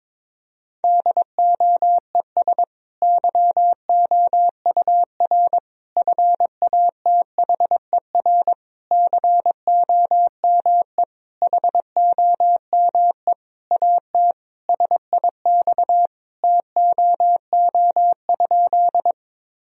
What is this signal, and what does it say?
Morse code, heard as a keyed tone, about 21 wpm, 700 Hz. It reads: DOES YOUR FATHER COME HOME AT SIX TOO?
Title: Sentences from Top 1000 Words - Review Part 2 - 22wpm